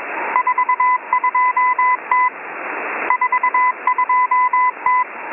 Morse code, heard as a keyed tone, about 21 wpm, 1000 Hz, with radio noise. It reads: 42T 42T
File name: M01- -1810utc-4589usb-msg-byDanix